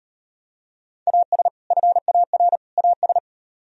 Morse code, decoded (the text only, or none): AS FAR AS